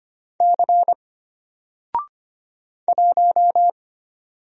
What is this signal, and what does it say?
Morse code, heard as a keyed tone, about 25 wpm, 700 Hz. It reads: C 1
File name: Single-Letter-Number-25wpm